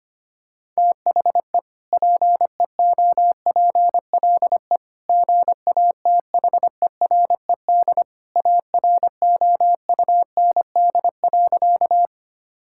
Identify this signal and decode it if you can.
Morse code, heard as a keyed tone, about 25 wpm, 700 Hz. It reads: THE PEOPLE GATHERED AROUND.
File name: practice / 25wpm / Sentences with 2-Word Phrases - 25wpm